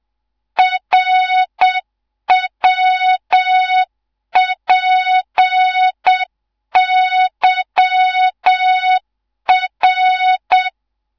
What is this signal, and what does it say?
Morse code, heard as a keyed tone, about 7 wpm, 3700 Hz. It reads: RWPYR